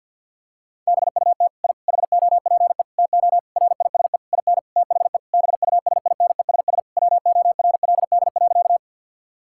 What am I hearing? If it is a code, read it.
Morse code, heard as a keyed tone, about 46 wpm, 700 Hz. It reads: BUT I HOPE TO RISE IN THE BUSINESS WORLD.
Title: Sentences from Top 1000 Words - Review Part 6 - 50wpm